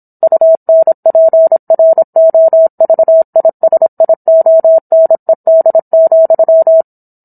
Morse code, heard as a keyed tone, about 27 wpm, 650 Hz. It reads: UNPROVISIONED,
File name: Book One / 17-Chapter 17 - 26 WPM